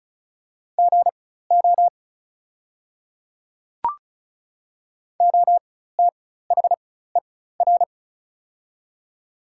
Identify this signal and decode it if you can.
Morse code, heard as a keyed tone, about 35 wpm, spaced as for 16 wpm, 700 Hz. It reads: GO OTHER
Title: 4x Character Spacing - Top 100 Words - 35wpm